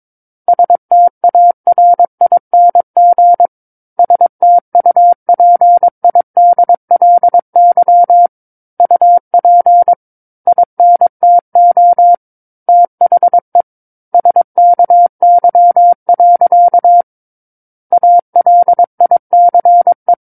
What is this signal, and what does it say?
Morse code, heard as a keyed tone, about 22 wpm, 700 Hz. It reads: STARING STUPIDLY UP INTO THE SKY. ALICE
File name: 06-Chapter-06 22 WPM